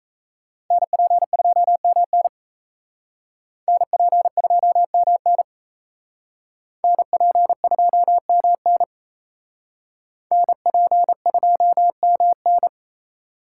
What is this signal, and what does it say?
Morse code, heard as a keyed tone, about 36 wpm, 700 Hz. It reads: NP2MN NP2MN NP2MN NP2MN